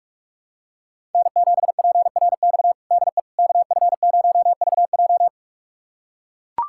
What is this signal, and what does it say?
Morse code, heard as a keyed tone, about 41 wpm, 700 Hz. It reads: N7PRX DE KF0VJ
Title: USA Callsign DE Callsign - 5 Characters - 45wpm